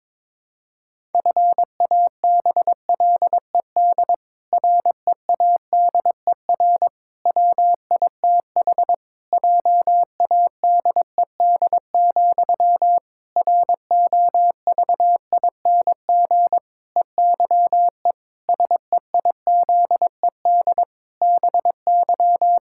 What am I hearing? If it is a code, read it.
Morse code, heard as a keyed tone, about 22 wpm, 700 Hz. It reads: FABLED READER WITH JADED, ROVING EYE SEIZED BY
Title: Pangram Challenge v2 - 22wpm